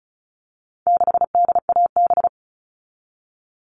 Morse code, heard as a keyed tone, about 34 wpm, 700 Hz. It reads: 6DAB
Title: All Cut Numbers - 4-Digit 35wpm